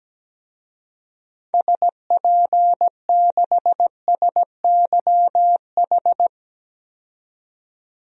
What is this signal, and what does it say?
Morse code, heard as a keyed tone, about 17 wpm, 700 Hz. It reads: SP6SYH
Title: DXCC-100 Call Signs - 6 Characters - 17wpm